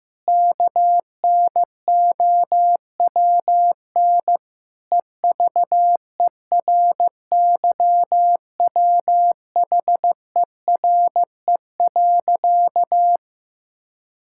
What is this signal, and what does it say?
Morse code, heard as a keyed tone, about 15 wpm, 700 Hz. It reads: KNOWN EVERYWHERE.
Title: Sentences from Top 1600 Words 15wpm